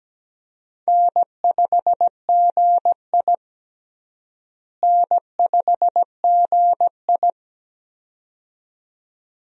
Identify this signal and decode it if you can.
Morse code, heard as a keyed tone, about 17 wpm, 700 Hz. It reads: N5GI N5GI
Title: USA Call Signs - 4 Characters - Repeated 2x - 17wpm